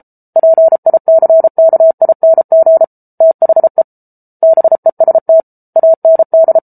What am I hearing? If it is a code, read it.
Morse code, heard as a keyed tone, about 34 wpm, 650 Hz. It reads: PICKING THE BEST AND